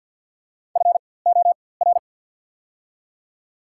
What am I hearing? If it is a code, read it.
Morse code, heard as a keyed tone, about 53 wpm, spaced as for 24 wpm, 690 Hz. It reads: FOR
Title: 4x Character Spacing - Top 300 Words - 50wpm